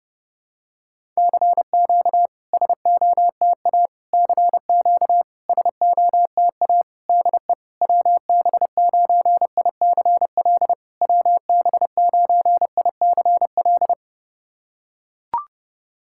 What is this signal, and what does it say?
Morse code, heard as a keyed tone, about 30 wpm, 690 Hz. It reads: CQ SOTA CQ SOTA DE WB9ICL WB9ICL
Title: SOTA QSO Practice 30wpm